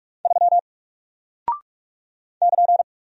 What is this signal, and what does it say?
Morse code, heard as a keyed tone, about 44 wpm, 700 Hz. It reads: 3 <KN>
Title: Single Character (Letter:Number:Punctuation:Prosign) Alt 45wpm